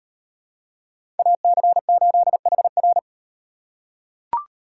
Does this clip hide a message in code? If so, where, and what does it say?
Yes: Morse code from 1.2 to 3.0 s, heard as a keyed tone, about 38 wpm, 700 Hz. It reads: AC8HR